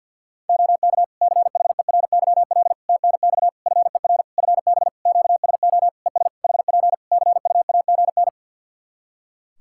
Morse code, heard as a keyed tone, about 51 wpm, 700 Hz. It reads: OK CHERYL TNX FER FB QSO ES HP CUAGN
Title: Practice QSO - Exchange 4 - Ending - 50wpm